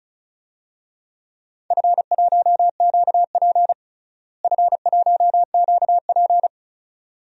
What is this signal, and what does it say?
Morse code, heard as a keyed tone, about 34 wpm, 700 Hz. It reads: F1QP F1QP
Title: DXCC-100 Call Signs - 4 Characters - Repeated 2x - 35wpm